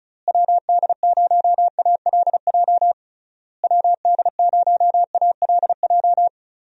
Morse code, heard as a keyed tone, about 34 wpm, 700 Hz. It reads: WD0ALJ WD0ALJ
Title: USA Call Signs - 6 Characters - Repeated 3x - 35wpm